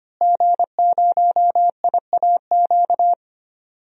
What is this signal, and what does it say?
Morse code, heard as a keyed tone, about 25 wpm, 700 Hz. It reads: G0IAQ
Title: DXCC-100 Call Signs - 5 Characters - Repeated 3x - 25wpm